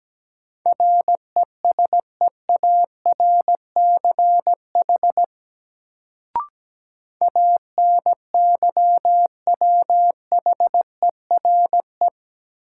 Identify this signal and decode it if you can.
Morse code, heard as a keyed tone, about 17 wpm, 700 Hz. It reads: RESEARCH ANYWHERE
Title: Sentences from Top 1400 Words - New Word Drill Alt - 17wpm